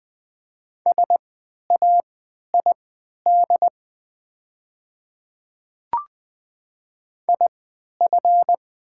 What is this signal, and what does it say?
Morse code, heard as a keyed tone, about 20 wpm, spaced as for 11 wpm, 700 Hz. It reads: SAID IF
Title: 3x Character Spacing - Top 100 Words - 20wpm